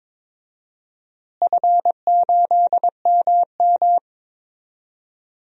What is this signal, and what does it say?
Morse code, heard as a keyed tone, about 22 wpm, 700 Hz. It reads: F8MM
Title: DXCC-100 Call Signs - 4 Characters - Repeated 3x - 22wpm